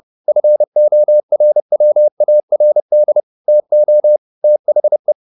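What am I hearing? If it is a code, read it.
Morse code, heard as a keyed tone, about 30 wpm, 600 Hz. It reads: FORWARD TO THE
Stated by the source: Morse code at 30 wpm, the novel Never by Ken Follett in English, from CW med Gnister.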